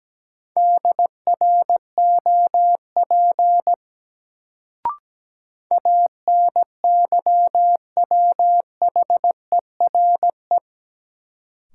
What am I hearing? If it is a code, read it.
Morse code, heard as a keyed tone, about 17 wpm, 700 Hz. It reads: DROP ANYWHERE
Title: Sentences from Top 1400 Words - New Word Drill - 17wpm